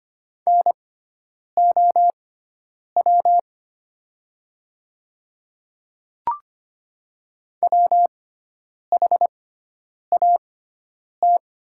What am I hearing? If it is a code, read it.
Morse code, heard as a keyed tone, about 25 wpm, spaced as for 9 wpm, 700 Hz. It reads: NOW WHAT
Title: 6x Character Spacing - Top 300 Words - 25wpm